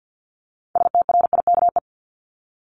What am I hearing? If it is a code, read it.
Morse code, heard as a keyed tone, about 50 wpm, 700 Hz. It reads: STRIKE
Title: Top 1501-1750 Words 50wpm